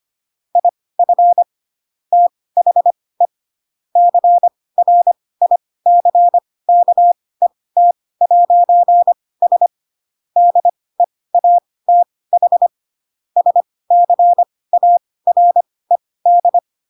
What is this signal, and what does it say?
Morse code, heard as a keyed tone, about 25 wpm, spaced as for 18 wpm, 700 Hz. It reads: IF THE CRICKET'S DEATH SCARED
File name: Pinocchio / Pinocchio ch5 12 25 ik1hge